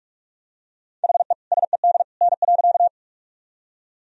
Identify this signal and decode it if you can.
Morse code, heard as a keyed tone, about 46 wpm, 700 Hz. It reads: HE SED N.
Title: Sentences from Top 100 Words and Abbreviations 45wpm